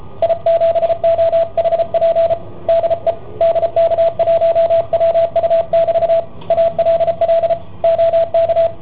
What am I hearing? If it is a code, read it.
Morse code, heard as a keyed tone, about 33 wpm, 670 Hz, with radio noise. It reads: I7OHP DE DK1WU<BT> ALL OK